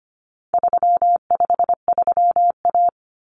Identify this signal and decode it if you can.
Morse code, heard as a keyed tone, about 25 wpm, 700 Hz. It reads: 353A
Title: Common Cut Numbers - 4-Digit 25wpm